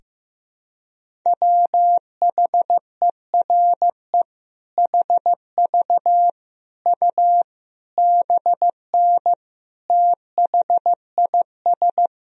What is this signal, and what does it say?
Morse code, heard as a keyed tone, about 15 wpm, 700 Hz. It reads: WHERE HV U BN THIS
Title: Sentences from Top 300 Words and Abbreviations 15wpm